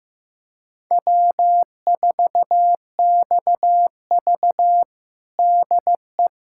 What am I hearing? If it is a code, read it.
Morse code, heard as a keyed tone, about 15 wpm, 700 Hz. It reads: W4XV DE
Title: USA Callsign DE Callsign - 4 Characters - 15wpm